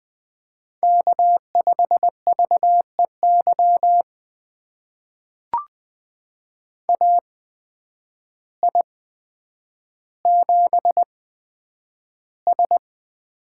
Morse code, heard as a keyed tone, about 20 wpm, spaced as for 9 wpm, 700 Hz. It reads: K5VEY AI7S
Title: USA Call Signs - 8x Character Spacing - 20wpm